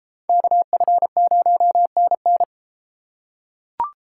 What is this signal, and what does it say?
Morse code, heard as a keyed tone, about 30 wpm, 700 Hz. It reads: KF0NN